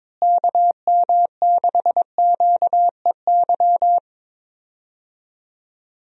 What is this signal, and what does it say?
Morse code, heard as a keyed tone, about 22 wpm, 700 Hz. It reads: KM6QEY
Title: USA Call Signs - 6 Characters - Repeated 2x - 22wpm